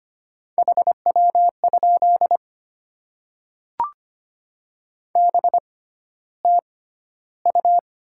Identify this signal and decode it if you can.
Morse code, heard as a keyed tone, about 25 wpm, spaced as for 13 wpm, 700 Hz. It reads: HW? BTU
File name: Top 20 QSO Elements - 6x Character Spacing - 25wpm